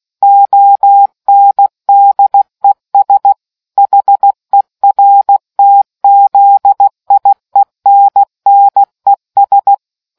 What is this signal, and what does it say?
Morse code, heard as a keyed tone, about 16 wpm, 800 Hz. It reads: ONDES HERTZIENNES